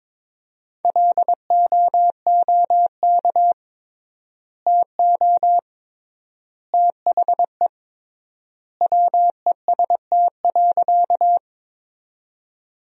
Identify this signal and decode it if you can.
Morse code, heard as a keyed tone, about 22 wpm, 700 Hz. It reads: LOOK TO THE WEST.
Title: 3x Word-Spacing - Sentences from Top 400 Words - 22wpm